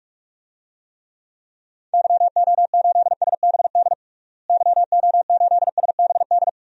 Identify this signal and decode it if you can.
Morse code, heard as a keyed tone, about 48 wpm, 700 Hz. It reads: YO8SBD YO8SBD
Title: DXCC-100 Call Signs - 6 Characters - Repeated 3x - 45wpm